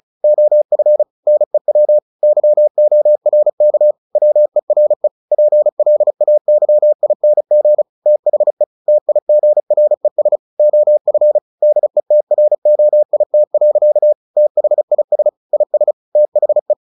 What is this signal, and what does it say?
Morse code, heard as a keyed tone, about 34 wpm, 600 Hz. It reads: OF NEW YORK WERE PLAYING THE TIGRES OF DETROIT. THIS IS THE
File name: the old man and the sea / the old man and the sea 1 35wpm